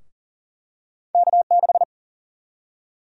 Morse code, heard as a keyed tone, about 40 wpm, 700 Hz. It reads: K6
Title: USA Call Sign Prefixes - 40wpm